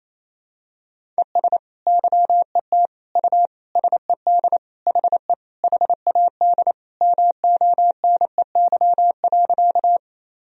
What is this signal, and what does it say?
Morse code, heard as a keyed tone, about 28 wpm, 700 Hz. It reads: ES YET U SED HE HAD MONEY.
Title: Sentences from Top 500 Words and Abbreviations 28wpm